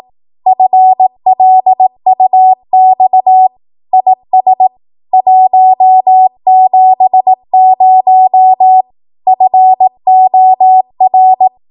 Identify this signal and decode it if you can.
Morse code, heard as a keyed tone, about 18 wpm, 750 Hz. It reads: FLUX IS 170 FOR